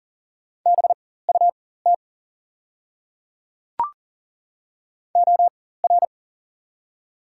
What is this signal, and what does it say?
Morse code, heard as a keyed tone, about 40 wpm, spaced as for 19 wpm, 700 Hz. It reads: BUT OR
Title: 4x Character Spacing - Top 100 Words - 40wpm